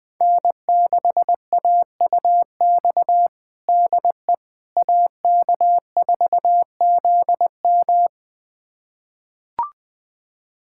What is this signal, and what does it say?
Morse code, heard as a keyed tone, about 20 wpm, 700 Hz. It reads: N6AUX DE AK4ZM
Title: USA Callsign DE Callsign - 5 Characters - 20wpm